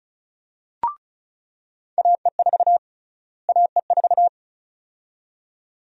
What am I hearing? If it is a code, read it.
Morse code, heard as a keyed tone, about 34 wpm, 700 Hz. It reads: AE4 AE4
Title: USA Call Sign Prefixes - Repeated 2x - 35wpm